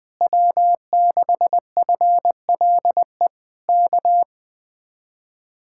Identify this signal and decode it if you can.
Morse code, heard as a keyed tone, about 20 wpm, 700 Hz. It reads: W6FLE K
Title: Speed-Racing Common QSO Phrases - 20wpm